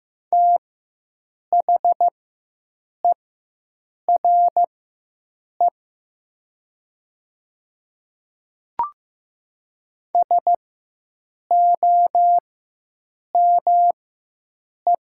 Morse code, heard as a keyed tone, about 15 wpm, spaced as for 7 wpm, 700 Hz. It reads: THERE SOME
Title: 4x Character Spacing - Top 100 Words - 15wpm